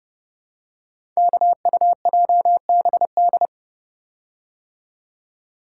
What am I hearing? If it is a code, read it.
Morse code, heard as a keyed tone, about 30 wpm, 700 Hz. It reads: KUJBD
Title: Code Groups - 5 Letter:Numbers - 30wpm